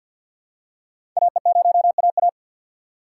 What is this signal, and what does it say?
Morse code, heard as a keyed tone, about 52 wpm, 700 Hz. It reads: AE0AA
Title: USA Call Signs - 5 Characters - 4x Character Spacing - 50wpm